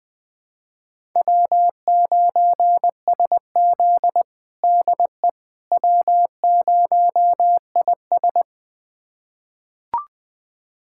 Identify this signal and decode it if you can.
Morse code, heard as a keyed tone, about 20 wpm, 700 Hz. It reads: W9SZ DE W0IS
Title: USA Callsign DE Callsign - 4 Characters - 20wpm